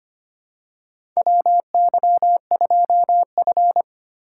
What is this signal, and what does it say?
Morse code, heard as a keyed tone, about 25 wpm, 700 Hz. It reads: WY2F
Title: USA Call Signs - 4 Characters - 4x Character Spacing - 25wpm